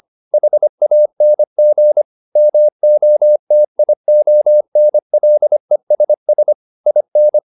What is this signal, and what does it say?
Morse code, heard as a keyed tone, about 25 wpm, 600 Hz. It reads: HANG MOTIONLESS IN